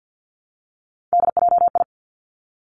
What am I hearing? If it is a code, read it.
Morse code, heard as a keyed tone, about 50 wpm, 700 Hz. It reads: DJI
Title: Three Challenging Letter-Number Combinations 50wpm